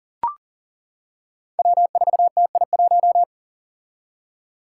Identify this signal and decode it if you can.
Morse code, heard as a keyed tone, about 40 wpm, 700 Hz. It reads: W4TI1
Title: Five Challenging Letter-Number Combinations 40wpm